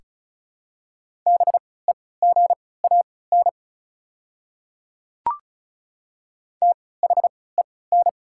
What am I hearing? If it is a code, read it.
Morse code, heard as a keyed tone, about 36 wpm, spaced as for 20 wpm, 690 Hz. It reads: BEGAN THEN
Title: 3x Character Spacing - Top 300 Words - 35wpm